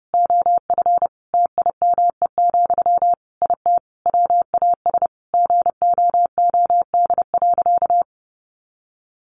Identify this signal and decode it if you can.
Morse code, heard as a keyed tone, about 30 wpm, 700 Hz. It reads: OF TIME, IT WAS GOOD.
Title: Sentences from Top 1300 Words 30wpm